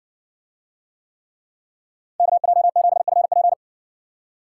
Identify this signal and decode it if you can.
Morse code, heard as a keyed tone, about 60 wpm, 700 Hz. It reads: KJ7FP